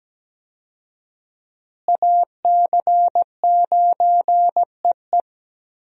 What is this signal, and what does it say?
Morse code, heard as a keyed tone, about 17 wpm, 700 Hz. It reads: AC9EE
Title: USA Call Signs - 5 Characters - Repeated 3x - 17wpm